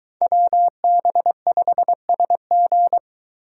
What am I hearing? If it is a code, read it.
Morse code, heard as a keyed tone, about 22 wpm, 700 Hz. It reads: WB5SG